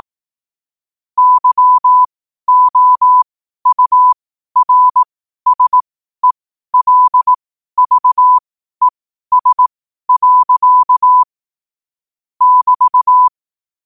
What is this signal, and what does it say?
Morse code, heard as a keyed tone, about 18 wpm, spaced as for 12 wpm, 1000 Hz. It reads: YOURSELVES. <BT>
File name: qotdfiles / 12wpm